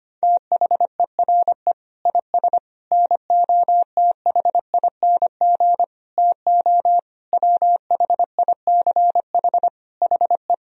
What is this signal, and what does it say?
Morse code, heard as a keyed tone, about 25 wpm, 700 Hz. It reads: THERE IS NOTHING TO WHICH HE
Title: Sentences from Top 1000 Words - Review Part 3 - 25wpm